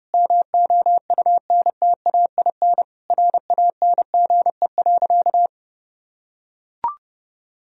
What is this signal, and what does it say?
Morse code, heard as a keyed tone, about 28 wpm, 700 Hz. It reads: MOUNTAIN RANGE.
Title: Sentences from Top 1000 Words - Review Part 4 - 30wpm